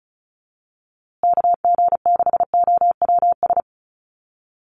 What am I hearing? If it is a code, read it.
Morse code, heard as a keyed tone, about 34 wpm, 700 Hz. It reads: KG6OWS
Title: USA Call Signs Repeated 3x 35wpm